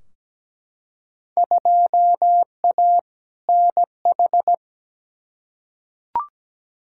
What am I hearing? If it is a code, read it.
Morse code, heard as a keyed tone, about 17 wpm, 700 Hz. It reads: 2A NH
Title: Field Day Exchanges Encore - 17wpm